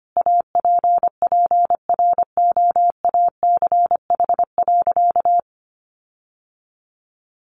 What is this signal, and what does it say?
Morse code, heard as a keyed tone, about 25 wpm, 700 Hz. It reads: APPROACH.
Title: Sentences from Top 1400 Words 25wpm